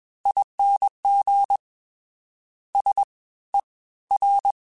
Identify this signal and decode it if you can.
Morse code, heard as a keyed tone, about 21 wpm, spaced as for 15 wpm, 800 Hz. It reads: ING SER